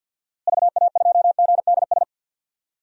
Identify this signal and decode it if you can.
Morse code, heard as a keyed tone, about 46 wpm, 700 Hz. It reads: VA2GDS
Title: DXCC-100 Call Signs - 6 Characters - Repeated 2x - 50wpm